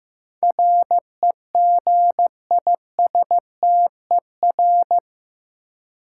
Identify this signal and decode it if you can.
Morse code, heard as a keyed tone, about 15 wpm, 700 Hz. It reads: REGISTER
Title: Sentences from Top 1200 Words - New Word Drill Alt - 15wpm